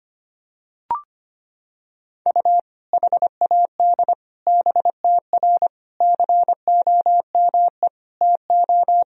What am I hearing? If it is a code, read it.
Morse code, heard as a keyed tone, about 25 wpm, 700 Hz. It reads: U HAD BTR COME TO
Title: Sentences from Top 400 Words and Abbreviations 25wpm